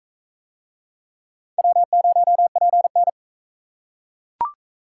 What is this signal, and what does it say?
Morse code, heard as a keyed tone, about 40 wpm, 700 Hz. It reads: W0PN